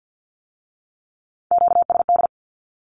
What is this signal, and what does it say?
Morse code, heard as a keyed tone, about 50 wpm, 700 Hz. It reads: QSD
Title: Q-Codes with Spoken Meaning - 50wpm